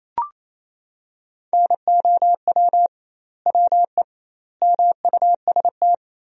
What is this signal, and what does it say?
Morse code, heard as a keyed tone, about 28 wpm, 700 Hz. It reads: NOW WE MUST
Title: 2x Word-Spacing - Sentences from Top 500 Words - 28wpm